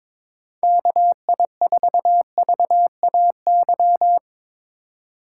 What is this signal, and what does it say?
Morse code, heard as a keyed tone, about 22 wpm, 700 Hz. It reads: KI4VAY
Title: USA Call Signs - 6 Characters - 3x Character Spacing - 22wpm